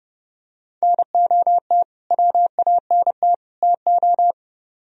Morse code, heard as a keyed tone, about 28 wpm, 700 Hz. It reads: NOT WANT TO